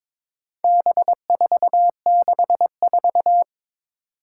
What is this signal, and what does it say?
Morse code, heard as a keyed tone, about 22 wpm, 700 Hz. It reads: B464
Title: Four Challenging Letter-Number Combinations 22wpm